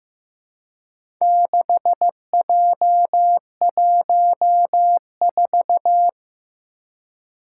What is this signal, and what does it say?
Morse code, heard as a keyed tone, about 15 wpm, 700 Hz. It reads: 6J14
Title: Four Challenging Letter-Number Combinations 15wpm